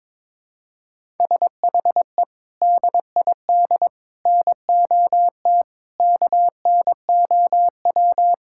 Morse code, heard as a keyed tone, about 22 wpm, 700 Hz. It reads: SHE DID NOT KNOW
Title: Sentences from Top 1000 Words - Review Part 6 - 22wpm